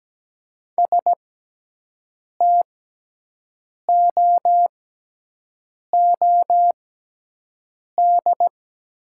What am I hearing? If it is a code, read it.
Morse code, heard as a keyed tone, about 17 wpm, spaced as for 6 wpm, 700 Hz. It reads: STOOD
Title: 6x Character Spacing - Top 500 Words - 17wpm